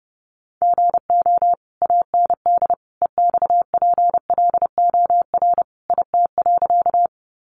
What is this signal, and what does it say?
Morse code, heard as a keyed tone, about 30 wpm, 700 Hz. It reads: GO AND EXPLOR IT.
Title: Sentences from Top 1100 Words 30wpm